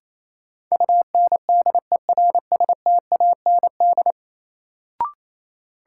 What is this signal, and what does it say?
Morse code, heard as a keyed tone, about 28 wpm, 700 Hz. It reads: UNDERSTAND
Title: Sentences from Top 600 Words - New Word Drill - 28wpm